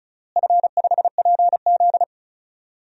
Morse code, heard as a keyed tone, about 34 wpm, 700 Hz. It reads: F5PZ
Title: DXCC-100 Call Signs - 4 Characters - Repeated 3x - 35wpm